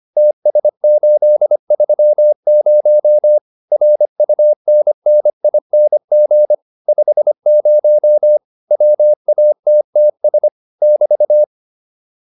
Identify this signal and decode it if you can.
Morse code, heard as a keyed tone, about 25 wpm, 600 Hz. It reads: TS830 RUNNING 50 WATTS <BT>
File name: QSO406 25